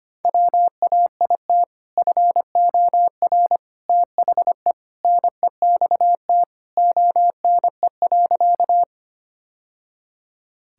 Morse code, heard as a keyed tone, about 25 wpm, 700 Hz. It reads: WAIT FOR THE NEXT ONE.
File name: practice / 25wpm / Sentences with 2-Word Phrases - 25wpm